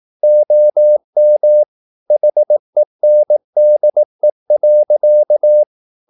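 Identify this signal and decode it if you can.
Morse code, heard as a keyed tone, about 18 wpm, 600 Hz. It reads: OM HENDE.
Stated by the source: Slow (QRS) Morse practice at 18 wpm, CW med Gnister.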